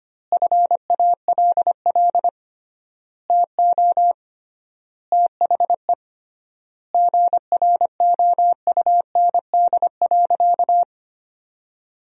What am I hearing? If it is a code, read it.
Morse code, heard as a keyed tone, about 25 wpm, 700 Hz. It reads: FALL TO THE GROUND.
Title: 3x Word-Spacing - Sentences from Top 500 Words - 25wpm